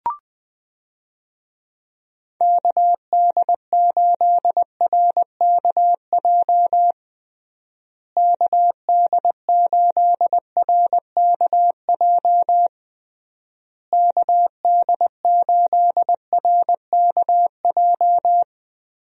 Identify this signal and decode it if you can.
Morse code, heard as a keyed tone, about 20 wpm, 700 Hz. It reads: KD8RKJ KD8RKJ KD8RKJ
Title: USA Call Signs - 6 Characters - Repeated 3x - 20wpm